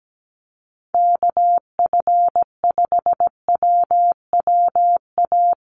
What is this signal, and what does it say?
Morse code, heard as a keyed tone, about 17 wpm, 700 Hz. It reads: KF5WWA
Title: USA Call Signs Repeated 3x 17wpm